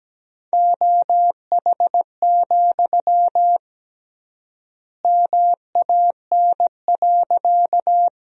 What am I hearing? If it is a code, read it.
Morse code, heard as a keyed tone, about 17 wpm, 700 Hz. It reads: OH, MAN.
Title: 3x Word-Spacing - Sentences from Top 500 Words - 17wpm